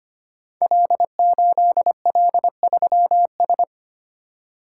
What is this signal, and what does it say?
Morse code, heard as a keyed tone, about 25 wpm, 700 Hz. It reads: L8L3S
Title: Code Groups - 5 Letter:Numbers - 25wpm